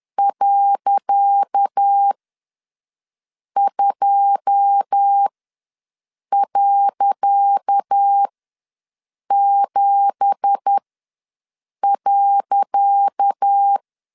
Morse code, heard as a keyed tone, about 10 wpm, spaced as for 6 wpm, 800 Hz. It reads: .2.7.
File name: LEZIONI / 10 SIGNS / 10C DOT ws 10wpm 4213